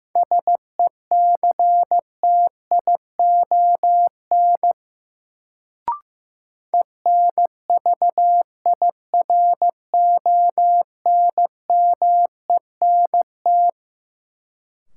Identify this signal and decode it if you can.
Morse code, heard as a keyed tone, about 15 wpm, 700 Hz. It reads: SECTION ENVIRONMENT
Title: Sentences from Top 1000 Words - Review Part 5 - New Word Drill - 15wpm